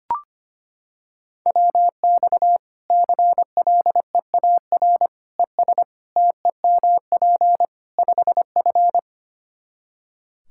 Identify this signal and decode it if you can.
Morse code, heard as a keyed tone, about 25 wpm, 700 Hz. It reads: WX CLEAR ES TEMP 5F
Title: Practice QSO - Exchange 2 - Info - 25wpm